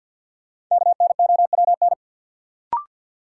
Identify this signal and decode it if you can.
Morse code, heard as a keyed tone, about 49 wpm, 700 Hz. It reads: KNOWN
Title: Sentences from Top 600 Words - New Word Drill - 50wpm